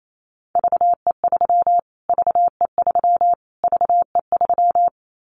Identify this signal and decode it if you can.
Morse code, heard as a keyed tone, about 28 wpm, 700 Hz. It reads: VE3 VE3 VE3
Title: Callsign Prefixes for Top 25 DXCC - Repeated 3x - 28wpm